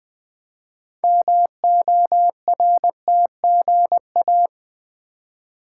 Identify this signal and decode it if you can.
Morse code, heard as a keyed tone, about 20 wpm, 700 Hz. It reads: MORTGA
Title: Build A Word - 2 Letters At A Time - 8-Letter Words - 20wpm